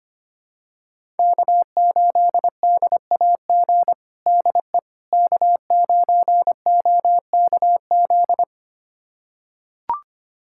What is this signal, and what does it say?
Morse code, heard as a keyed tone, about 25 wpm, 700 Hz. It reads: K8DAG DE K9OKZ
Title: USA Callsign DE Callsign - 5 Characters - 25wpm